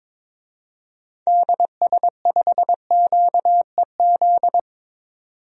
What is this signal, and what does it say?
Morse code, heard as a keyed tone, about 22 wpm, 700 Hz. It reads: DS5QEZ
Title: DXCC-100 Call Signs - 6 Characters - 22wpm